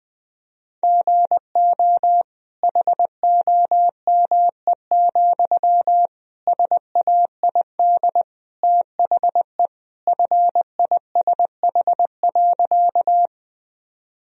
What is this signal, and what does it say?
Morse code, heard as a keyed tone, about 20 wpm, 700 Hz. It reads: GO HOME, SAID THE FISH.
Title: Sentences from Top 1000 Words - Review Part 1 - 20wpm